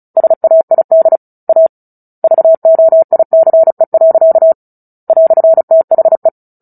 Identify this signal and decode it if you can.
Morse code, heard as a keyed tone, about 34 wpm, 650 Hz. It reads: SAID A VOICE. "THE
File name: Book Two / 09-Chapter 9 - 35 WPM